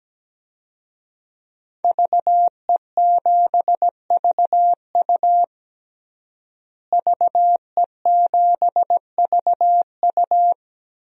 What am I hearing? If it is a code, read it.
Morse code, heard as a keyed tone, about 17 wpm, 700 Hz. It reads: VE7VU VE7VU
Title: DXCC-100 Call Signs - 5 Characters - Repeated 3x - 17wpm